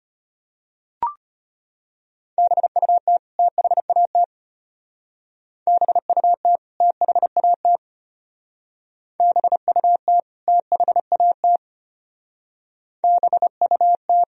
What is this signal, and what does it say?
Morse code, heard as a keyed tone, about 32 wpm, 700 Hz. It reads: BUT THAT BUT THAT BUT THAT BUT